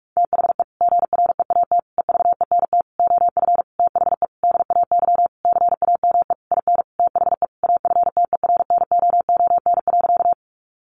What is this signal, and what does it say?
Morse code, heard as a keyed tone, about 44 wpm, 700 Hz. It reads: THE GREAT EVENT OF THE DAY CAME IN THE AFTERNOON.
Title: Sentences from Top 1500 Words 45wpm